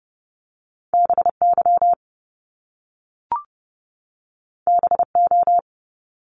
Morse code, heard as a keyed tone, about 30 wpm, 700 Hz. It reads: BY BO
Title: Top Two Letter Combinations in Words 30wpm